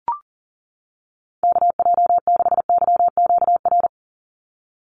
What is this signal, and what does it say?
Morse code, heard as a keyed tone, about 39 wpm, 700 Hz. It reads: KJ6YQR
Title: USA Call Signs 40wpm